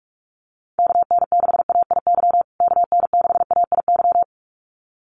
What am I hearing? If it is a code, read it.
Morse code, heard as a keyed tone, about 44 wpm, 700 Hz. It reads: KN6AIY KN6AIY
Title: USA Call Signs Repeated 2x 45wpm